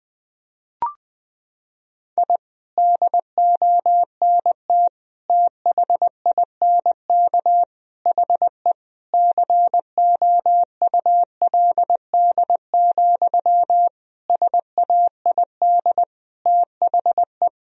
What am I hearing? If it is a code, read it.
Morse code, heard as a keyed tone, about 21 wpm, 700 Hz. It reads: I DONT THINK HE COULD, SAID THE
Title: Sentences from Top 1000 Words - Review Part 4 - 20wpm